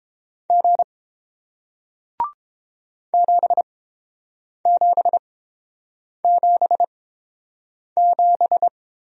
Morse code, heard as a keyed tone, about 30 wpm, spaced as for 7 wpm, 700 Hz. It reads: G 7777